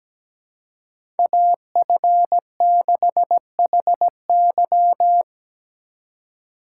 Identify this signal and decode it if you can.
Morse code, heard as a keyed tone, about 17 wpm, 700 Hz. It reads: AF6HY
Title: USA Call Signs - 4x Character Spacing - 17wpm